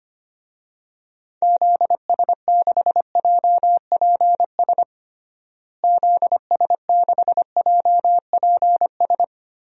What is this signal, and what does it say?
Morse code, heard as a keyed tone, about 25 wpm, 700 Hz. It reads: ZS6JPS ZS6JPS
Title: DXCC-100 Call Signs - 6 Characters - Repeated 3x - 25wpm